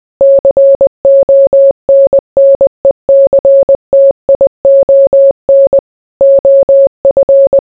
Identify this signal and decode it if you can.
Morse code, heard as a keyed tone, about 20 wpm, 550 Hz. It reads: CONNECTION OF